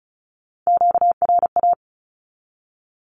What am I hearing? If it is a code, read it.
Morse code, heard as a keyed tone, about 34 wpm, 700 Hz. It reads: QRA
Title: Q-Codes 35wpm